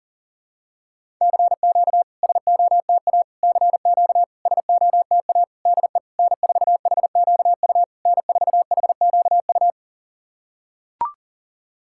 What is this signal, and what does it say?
Morse code, heard as a keyed tone, about 37 wpm, 690 Hz. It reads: CQ SOTA CQ SOTA DE N4HQU N4HQU
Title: SOTA QSO Practice 40wpm